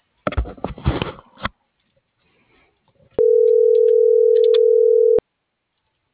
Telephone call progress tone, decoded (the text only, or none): ringback tone